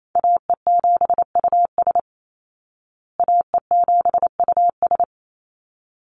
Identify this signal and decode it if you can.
Morse code, heard as a keyed tone, about 28 wpm, 700 Hz. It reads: AE7US AE7US
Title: Call Signs Repeated 3x - 28wpm